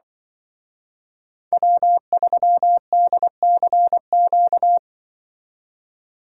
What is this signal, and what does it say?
Morse code, heard as a keyed tone, about 25 wpm, 700 Hz. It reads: W3DCQ
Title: Speed Racing - USA 1x3 Call Signs 20wpm